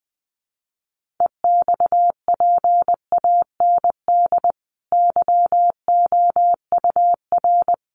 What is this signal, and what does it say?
Morse code, heard as a keyed tone, about 20 wpm, 700 Hz. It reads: EXPAND YOUR
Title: Sentences from Top 1400 Words 20wpm